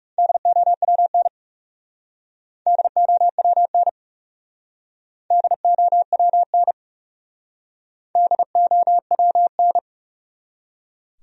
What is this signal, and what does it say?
Morse code, heard as a keyed tone, about 38 wpm, 700 Hz. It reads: DOWN DOWN DOWN DOWN
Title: Speed-Racing Top 200 Words 30wpm